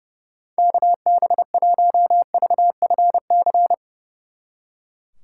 Morse code, heard as a keyed tone, about 37 wpm, 700 Hz. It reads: KB1VFC
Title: Speed Racing - USA 2x3 Call Signs 30wpm